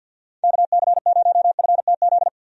Morse code, heard as a keyed tone, about 46 wpm, 700 Hz. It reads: KC0FTZ
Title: USA Call Signs - 6 Characters - 50wpm